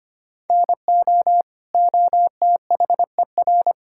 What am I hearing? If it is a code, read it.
Morse code, heard as a keyed tone, about 25 wpm, 700 Hz. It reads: NO OTHER
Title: Sentences from Top 1000 Words - Review Part 5 - 25wpm